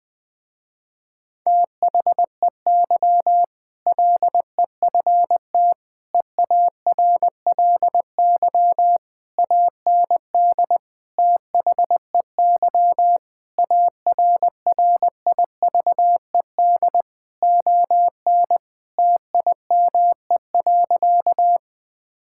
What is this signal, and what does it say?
Morse code, heard as a keyed tone, about 21 wpm, 700 Hz. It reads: THEY LEFT EARLY AND THEY ARRIVED ON TIME.
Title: Sentences with 2-Word Phrases - Spoken Beginning - 20wpm